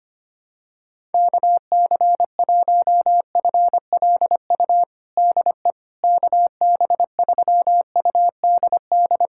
Morse code, heard as a keyed tone, about 25 wpm, 700 Hz. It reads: KC1FLU DE KB3UDD
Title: USA Callsign DE Callsign - 6 Characters - 25wpm